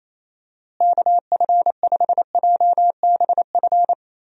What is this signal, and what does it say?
Morse code, heard as a keyed tone, about 28 wpm, 700 Hz. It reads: KF5JBF
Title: USA Call Signs - 6 Characters - Repeated 2x - 28wpm